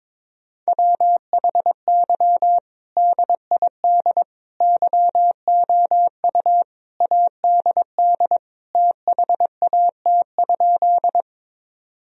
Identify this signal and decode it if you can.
Morse code, heard as a keyed tone, about 22 wpm, 700 Hz. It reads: WHY DID YOU ADD THAT?
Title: Sentences from Top 1000 Words - Review Part 1 - 22wpm